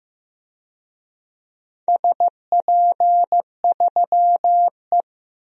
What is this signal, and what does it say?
Morse code, heard as a keyed tone, about 15 wpm, 700 Hz. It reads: SP3E
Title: DXCC-100 Call Signs - 4 Characters - Repeated 2x - 15wpm